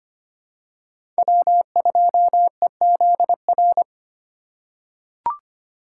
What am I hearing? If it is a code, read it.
Morse code, heard as a keyed tone, about 25 wpm, 700 Hz. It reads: W2EZR
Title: USA Call Signs - 5 Characters - 3x Character Spacing - 25wpm